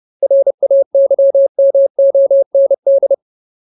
Morse code, heard as a keyed tone, about 30 wpm, 550 Hz. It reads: RAYMOND
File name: Ultimate Word List / 30WPM Ultimate